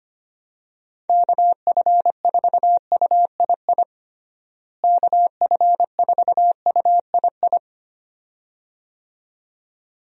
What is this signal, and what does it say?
Morse code, heard as a keyed tone, about 25 wpm, 700 Hz. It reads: KF4UII KF4UII
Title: USA Call Signs - 6 Characters - Repeated 2x - 25wpm